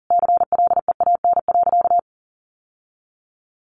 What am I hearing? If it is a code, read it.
Morse code, heard as a keyed tone, about 37 wpm, 700 Hz. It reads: CLEAN.
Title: Sentences from Top 1100 Words 40wpm